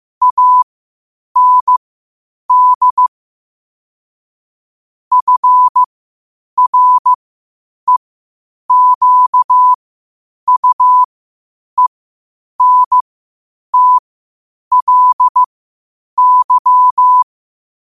Morse code, heard as a keyed tone, about 15 wpm, spaced as for 8 wpm, 1000 Hz. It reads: AND FREQUENTLY
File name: qotdfiles / 8wpm